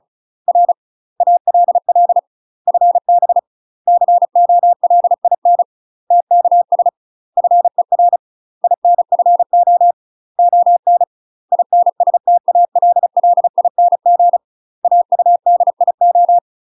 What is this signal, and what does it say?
Morse code, heard as a keyed tone, about 36 wpm, 700 Hz. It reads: R ALL FB COLIN TKS FER INFO ON INSTALLING AUDIO
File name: UK MASTER QSOs TRAINING / 35WPM / UK MASTER QSOs 0014